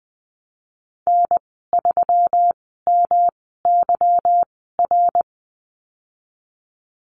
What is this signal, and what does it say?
Morse code, heard as a keyed tone, about 20 wpm, spaced as for 15 wpm, 700 Hz. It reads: N3MYR
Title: 179 Lesson 29g - Slash - Call Signs 20wpm